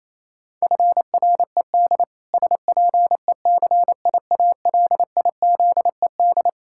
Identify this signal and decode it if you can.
Morse code, heard as a keyed tone, about 28 wpm, 700 Hz. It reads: FRED SPECIALIZED
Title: Pangram Challenge v2 - 28wpm